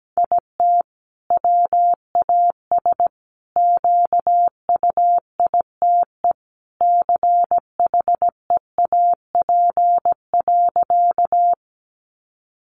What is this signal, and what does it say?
Morse code, heard as a keyed tone, about 17 wpm, 700 Hz. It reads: IT WAS QUITE CHEAP.
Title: Sentences from Top 1200 Words 17wpm